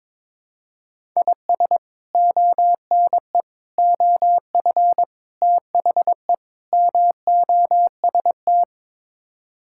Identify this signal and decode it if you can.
Morse code, heard as a keyed tone, about 22 wpm, 700 Hz. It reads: IS ONE OF THE MOST
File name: practice / 22wpm / Build A Phrase - 5-Word Phrases - 22wpm